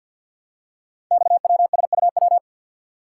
Morse code, heard as a keyed tone, about 50 wpm, 700 Hz. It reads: XWSUW
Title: Code Groups - 5 Letters - 50wpm